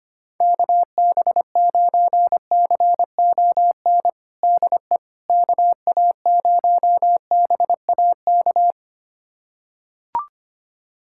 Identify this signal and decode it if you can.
Morse code, heard as a keyed tone, about 25 wpm, 700 Hz. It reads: KB9CON DE KA0BAK
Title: USA Callsign DE Callsign 25wpm